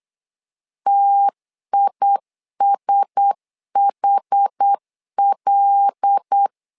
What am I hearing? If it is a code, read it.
Morse code, heard as a keyed tone, about 8 wpm, 800 Hz. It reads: TISHL
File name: LEZIONI / 5 LBVC / 5 LBVC 8wpm 2935